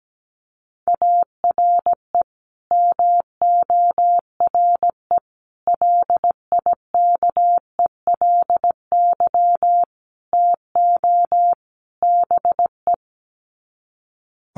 Morse code, heard as a keyed tone, about 17 wpm, 700 Hz. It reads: ARE MORE LIKELY TO BE
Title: Sets of 5 Words 17wpm